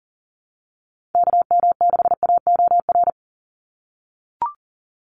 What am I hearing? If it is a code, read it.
Morse code, heard as a keyed tone, about 41 wpm, 700 Hz. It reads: KM6AOR
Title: USA Call Signs 40wpm